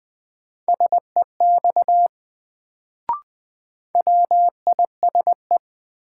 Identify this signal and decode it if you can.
Morse code, heard as a keyed tone, about 20 wpm, 700 Hz. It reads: SEX WISE
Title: Sentences from Top 1600 Words - New Word Drill - 20wpm